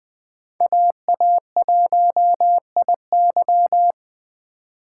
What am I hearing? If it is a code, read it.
Morse code, heard as a keyed tone, about 20 wpm, 700 Hz. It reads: AA1IY